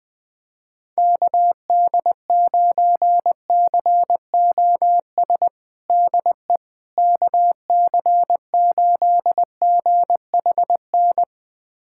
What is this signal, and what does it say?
Morse code, heard as a keyed tone, about 20 wpm, 700 Hz. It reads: KD9COS DE KC8GHN
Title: USA Callsign DE Callsign - 6 Characters - 20wpm